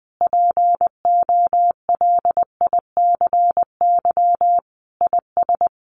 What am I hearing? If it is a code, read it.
Morse code, heard as a keyed tone, about 20 wpm, 700 Hz. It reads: POLICY IS
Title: Sentences from Top 1400 Words 20wpm